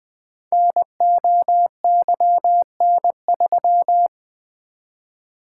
Morse code, heard as a keyed tone, about 20 wpm, 700 Hz. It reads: NOYN3
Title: Code Groups - 5 Letter:Numbers - 20wpm